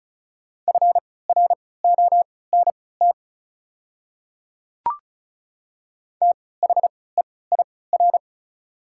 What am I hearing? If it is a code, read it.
Morse code, heard as a keyed tone, about 36 wpm, spaced as for 20 wpm, 690 Hz. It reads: FRONT THEIR
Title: 3x Character Spacing - Top 500 Words - 35wpm